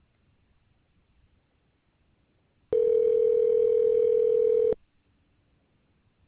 Ringback tone (North America), 440 + 480 Hz.